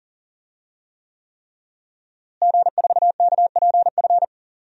Morse code, heard as a keyed tone, about 40 wpm, 700 Hz. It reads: G4KPF
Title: DXCC-100 Call Signs - 5 Characters - Repeated 3x - 40wpm